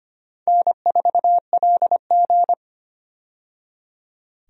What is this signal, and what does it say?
Morse code, heard as a keyed tone, about 25 wpm, 700 Hz. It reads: N4LG
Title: USA Call Signs - 4 Characters - 25wpm